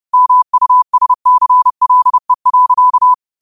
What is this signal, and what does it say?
Morse code, heard as a keyed tone, about 30 wpm, 1000 Hz. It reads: MUSCLE.